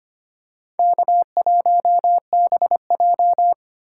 Morse code, heard as a keyed tone, about 25 wpm, 700 Hz. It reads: K1BJ
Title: USA Call Signs - 3x Character Spacing - 25wpm